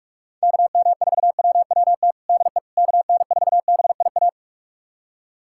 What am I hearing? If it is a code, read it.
Morse code, heard as a keyed tone, about 46 wpm, 690 Hz. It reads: KM4WWT DE KN4BIA
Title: USA Callsign DE Callsign 45wpm